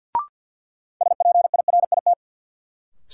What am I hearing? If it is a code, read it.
Morse code, heard as a keyed tone, about 51 wpm, 700 Hz. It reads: SPIRIT
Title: Sentences from Top 1400 Words - New Word Drill Alt - 50wpm